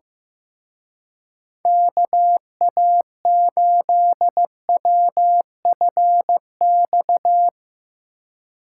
Morse code, heard as a keyed tone, about 17 wpm, 700 Hz. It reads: KA8WFX
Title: Speed Racing - USA 2x3 Call Signs 15wpm